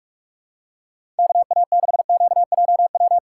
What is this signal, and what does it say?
Morse code, heard as a keyed tone, about 44 wpm, 700 Hz. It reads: KA6QJW
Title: USA Call Signs - 2x Character Spacing - 45wpm